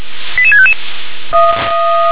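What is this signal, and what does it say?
Five-tone selective call (ZVEI), heard as quick runs of five tones, tones of about 70 ms. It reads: 88655
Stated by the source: ZVEI five-tone call received on SDR.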